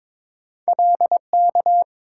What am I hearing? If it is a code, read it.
Morse code, heard as a keyed tone, about 22 wpm, 700 Hz. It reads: LK
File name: practice / 22wpm / Two Letter-Number Combinations - 22wpm